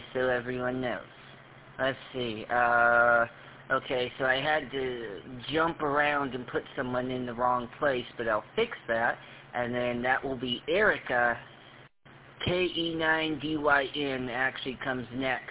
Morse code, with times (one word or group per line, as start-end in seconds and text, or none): none